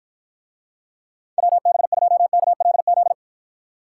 Morse code, heard as a keyed tone, about 52 wpm, 700 Hz. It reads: WB2KLZ